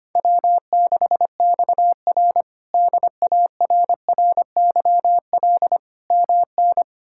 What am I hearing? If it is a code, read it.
Morse code, heard as a keyed tone, about 25 wpm, 700 Hz. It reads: W6XR DARRYL MN
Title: K1USN SST QSO Practice - 25wpm